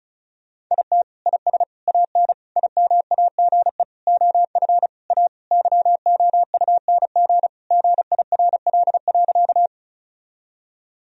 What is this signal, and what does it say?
Morse code, heard as a keyed tone, about 36 wpm, 700 Hz. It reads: IT IS AN IMAGE OF A YOUNG GIRL.
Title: Sentences from Top 1000 Words - Review Part 4 - 35wpm